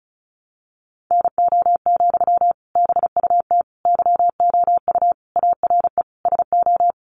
Morse code, heard as a keyed tone, about 34 wpm, 700 Hz. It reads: NO, BUT YOU ARE SO
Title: Sentences from Top 1100 Words 35wpm